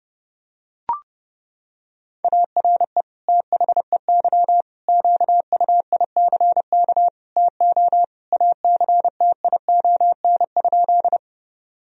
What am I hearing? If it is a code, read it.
Morse code, heard as a keyed tone, about 30 wpm, 700 Hz. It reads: ARE THEY QUICK TO ACTION?
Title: Sentences from Top 1000 Words - Review Part 4 - 30wpm